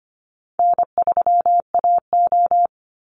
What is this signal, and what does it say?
Morse code, heard as a keyed tone, about 25 wpm, 700 Hz. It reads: N3AO
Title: Easy Call Signs - 25wpm